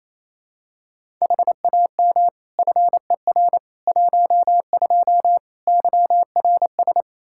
Morse code, heard as a keyed tone, about 28 wpm, 700 Hz. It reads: HAM FER 12 YRS
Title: Practice QSO - Exchange 3 - More Info - 28wpm